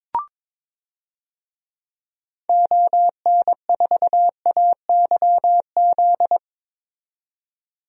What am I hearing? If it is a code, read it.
Morse code, heard as a keyed tone, about 22 wpm, 700 Hz. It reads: ON4AYZ
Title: DXCC-100 Call Signs - 6 Characters - 22wpm